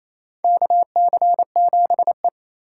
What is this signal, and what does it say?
Morse code, heard as a keyed tone, about 27 wpm, 700 Hz. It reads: KC7E
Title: USA Call Signs - 4x Character Spacing - 28wpm